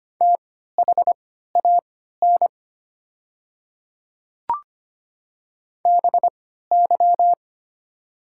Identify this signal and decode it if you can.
Morse code, heard as a keyed tone, about 25 wpm, spaced as for 14 wpm, 690 Hz. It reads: THAN BY
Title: 3x Character Spacing - Top 200 Words - 25wpm